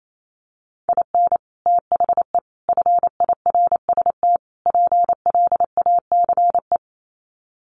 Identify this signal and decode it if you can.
Morse code, heard as a keyed tone, about 28 wpm, 700 Hz. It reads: IN THE FIRST PLACE
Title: Sets of 4 Words 28wpm